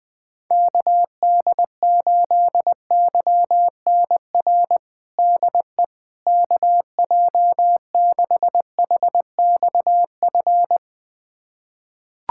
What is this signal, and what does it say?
Morse code, heard as a keyed tone, about 20 wpm, 700 Hz. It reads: KD8YNR DE KJ6HXF
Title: USA Callsign DE Callsign 20wpm